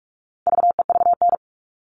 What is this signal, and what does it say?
Morse code, heard as a keyed tone, about 44 wpm, 700 Hz. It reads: VEVN
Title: All Cut Numbers - 4-Digit 45wpm